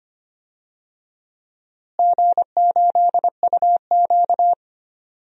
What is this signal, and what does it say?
Morse code, heard as a keyed tone, about 25 wpm, 700 Hz. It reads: G8UQ
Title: DXCC-100 Call Signs - 4 Characters - Repeated 2x - 25wpm